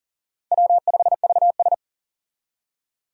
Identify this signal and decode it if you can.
Morse code, heard as a keyed tone, about 40 wpm, 700 Hz. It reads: W5VS
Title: USA Call Signs - 4 Characters - 40wpm